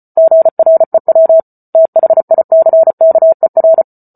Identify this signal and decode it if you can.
Morse code, heard as a keyed tone, about 34 wpm, 650 Hz. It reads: GREW THICKER